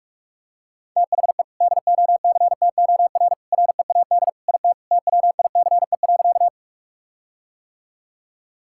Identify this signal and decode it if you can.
Morse code, heard as a keyed tone, about 41 wpm, 700 Hz. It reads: THE DOCTOR READ IT TWICE.
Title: Sentences from Top 1600 Words 45wpm